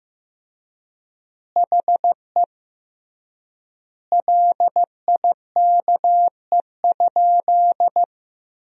Morse code, heard as a keyed tone, about 15 wpm, 700 Hz. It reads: HE LIKE?
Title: 3x Word-Spacing - Sentences from Top 100 Words - 15wpm